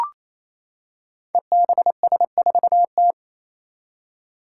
Morse code, heard as a keyed tone, about 28 wpm, 700 Hz. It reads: EBS4T
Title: Five Challenging Letter-Number Combinations 28wpm